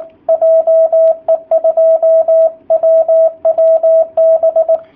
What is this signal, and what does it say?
Morse code, heard as a keyed tone, about 19 wpm, 660 Hz. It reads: JE2WWB
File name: jh8cbh / 2019cw / sp18- w